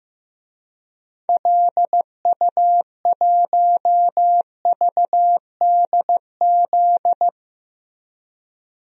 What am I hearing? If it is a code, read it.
Morse code, heard as a keyed tone, about 15 wpm, 700 Hz. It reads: LU1VDZ
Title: DXCC-100 Call Signs - 6 Characters - 15wpm